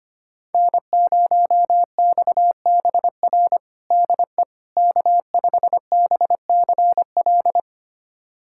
Morse code, heard as a keyed tone, about 25 wpm, 700 Hz. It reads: N0XBR DE K5BCL
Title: USA Callsign DE Callsign - 5 Characters - 25wpm